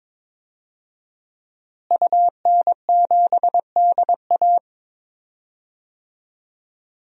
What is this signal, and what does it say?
Morse code, heard as a keyed tone, about 22 wpm, 700 Hz. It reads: UN7DA
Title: DXCC-100 Call Signs - 5 Characters - Repeated 2x - 22wpm